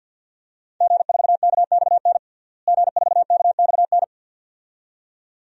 Morse code, heard as a keyed tone, about 50 wpm, 700 Hz. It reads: G4KXN G4KXN
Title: DXCC-100 Call Signs - 5 Characters - Repeated 2x - 50wpm